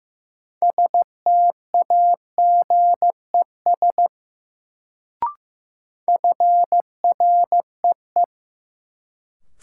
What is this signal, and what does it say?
Morse code, heard as a keyed tone, about 15 wpm, 700 Hz. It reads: STAGES FREE
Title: Sets of 5 Words Encore - New Word Drill - 15wpm